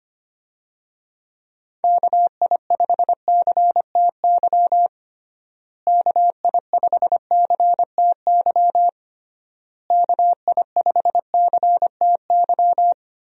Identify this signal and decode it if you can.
Morse code, heard as a keyed tone, about 25 wpm, 700 Hz. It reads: KI5CTY KI5CTY KI5CTY
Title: USA Call Signs - 6 Characters - Repeated 3x - 25wpm